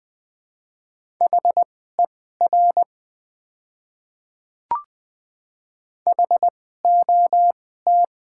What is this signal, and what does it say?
Morse code, heard as a keyed tone, about 20 wpm, spaced as for 14 wpm, 700 Hz. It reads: HER HOT
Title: 2x Character Spacing - Top 100 Words - 20wpm